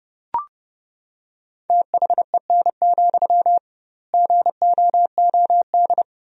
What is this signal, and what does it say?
Morse code, heard as a keyed tone, about 30 wpm, 700 Hz. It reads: THEN, GOOD
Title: 2x Word-Spacing - Sentences from Top 300 Words - 30wpm